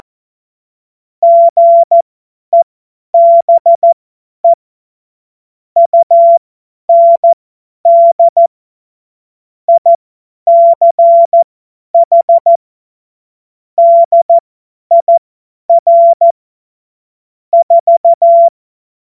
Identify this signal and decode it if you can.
Morse code, heard as a keyed tone, about 14 wpm, spaced as for 10 wpm, 680 Hz. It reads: GEBE UND ICH DIR 4
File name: Morsekurs / Witze / Witz 17. August 2021, 50 F70